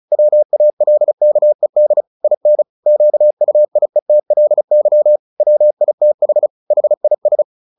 Morse code, heard as a keyed tone, about 34 wpm, 600 Hz. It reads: WALKED IN QUIETLY WITH HIS
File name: the old man and the sea / the old man and the sea 1 35wpm